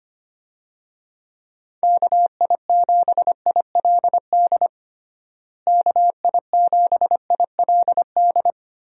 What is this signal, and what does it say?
Morse code, heard as a keyed tone, about 25 wpm, 700 Hz. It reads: KI7ILD KI7ILD
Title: USA Call Signs - 6 Characters - Repeated 3x - 25wpm